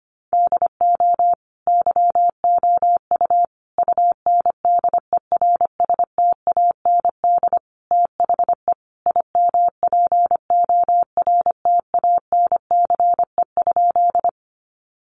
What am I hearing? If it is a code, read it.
Morse code, heard as a keyed tone, about 25 wpm, 700 Hz. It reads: DO YOU UNDERSTAND THE IMPORTANCE?
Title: Sentences from Top 1200 Words 25wpm